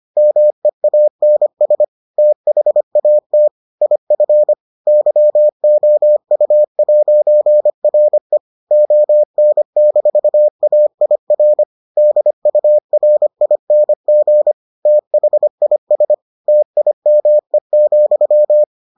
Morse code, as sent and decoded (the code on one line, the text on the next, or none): -- . .- -. ... / - .... .- - / .. ..-. / -.-- --- ..- .----. .-. / --- -. -....- .- .. .-. / -.. ..- .-. .. -. --. / - .... .. ... / - .. -- . --..--
MEANS THAT IF YOU'RE ON-AIR DURING THIS TIME,